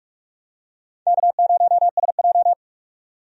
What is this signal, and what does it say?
Morse code, heard as a keyed tone, about 45 wpm, 700 Hz. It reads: K0SJ
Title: USA Call Signs - 4 Characters - 45wpm